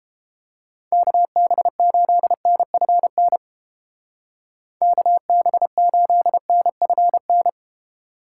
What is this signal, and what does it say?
Morse code, heard as a keyed tone, about 29 wpm, 700 Hz. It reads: KB8NFN KB8NFN